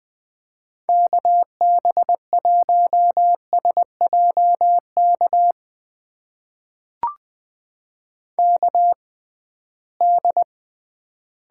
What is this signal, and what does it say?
Morse code, heard as a keyed tone, about 21 wpm, spaced as for 15 wpm, 700 Hz. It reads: KB1SJK KD
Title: USA Call Signs - 6 Characters - 6x Character Spacing - 20wpm